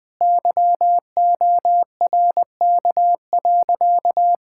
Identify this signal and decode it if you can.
Morse code, heard as a keyed tone, about 20 wpm, 700 Hz. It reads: YORK.
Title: Sentences with 2-Word Phrases - Spoken Beginning - 20wpm